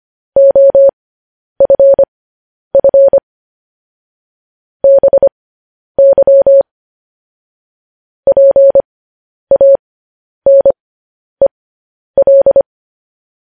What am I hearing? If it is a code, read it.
Morse code, heard as a keyed tone, about 25 wpm, spaced as for 10 wpm, 550 Hz. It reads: OFF BY PANEL
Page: news-headlines-in-morse-code-at-10-wpm